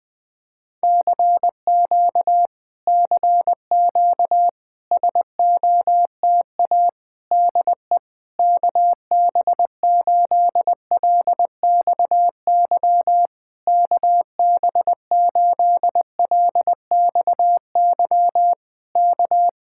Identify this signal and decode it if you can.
Morse code, heard as a keyed tone, about 20 wpm, 700 Hz. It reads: CQ CQ SOTA DE KB8LXY KB8LXY K
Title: Speed-Racing Common QSO Phrases Encore - 17wpm